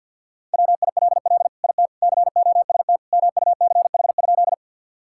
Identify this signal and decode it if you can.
Morse code, heard as a keyed tone, about 50 wpm, 700 Hz. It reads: WILL IT COST MUCH?
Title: Sentences from Top 1000 Words - Review Part 5 - 50wpm